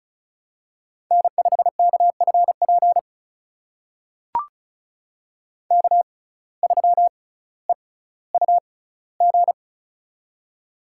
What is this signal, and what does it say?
Morse code, heard as a keyed tone, about 36 wpm, spaced as for 18 wpm, 700 Hz. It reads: N5KFP K3EUG
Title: USA Call Signs - 5 Characters - 6x Character Spacing - 35wpm